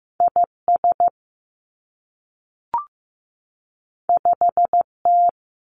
Morse code, heard as a keyed tone, about 15 wpm, 700 Hz. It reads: I S 5 T
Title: Two Challenging Letter-Number Combinations 15wpm